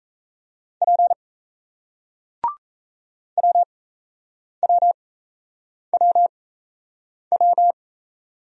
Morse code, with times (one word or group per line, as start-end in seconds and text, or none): 0.8-1.1 P
3.4-7.7 WWWW